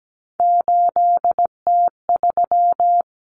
Morse code, heard as a keyed tone, about 17 wpm, 700 Hz. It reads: 8T3
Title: Common Cut Numbers - 3-Digit 17wpm